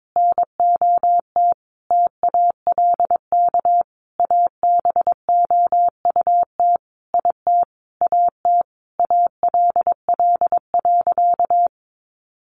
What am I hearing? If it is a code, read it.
Morse code, heard as a keyed tone, about 22 wpm, 700 Hz. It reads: NOT TALK ABOUT IT AT ALL.
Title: Sentences from Top 800 Words 22wpm